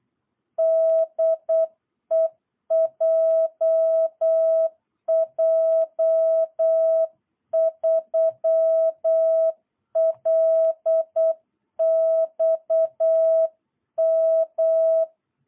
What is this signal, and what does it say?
Morse code, heard as a keyed tone, about 8 wpm, 660 Hz. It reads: DE JJ3LXM
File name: jh8cbh / 2022cw